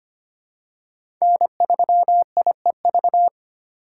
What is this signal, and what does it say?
Morse code, heard as a keyed tone, about 25 wpm, 700 Hz. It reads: N3IEV
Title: USA Call Signs - 5 Characters - 3x Character Spacing - 25wpm